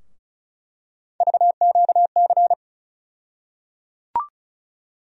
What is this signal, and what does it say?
Morse code, heard as a keyed tone, about 36 wpm, 700 Hz. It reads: VQC
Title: USA Call Sign Postfixes - 3 Letters 35wpm